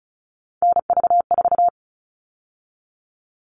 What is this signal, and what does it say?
Morse code, heard as a keyed tone, about 34 wpm, 700 Hz. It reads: NV4
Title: All Cut Numbers - 3-Digit 35wpm